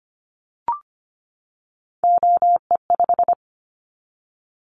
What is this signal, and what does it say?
Morse code, heard as a keyed tone, about 25 wpm, 700 Hz. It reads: OE5
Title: Callsign Prefixes for Top 25 DXCC - 25wpm